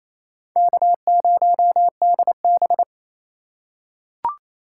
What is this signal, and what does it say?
Morse code, heard as a keyed tone, about 27 wpm, 700 Hz. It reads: K0DB
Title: USA Call Signs - 4 Characters - 4x Character Spacing - 28wpm